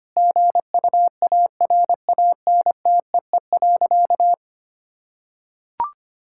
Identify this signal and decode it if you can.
Morse code, heard as a keyed tone, about 25 wpm, 700 Hz. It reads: GUARANTEE.
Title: Sentences from Top 1600 Words 25wpm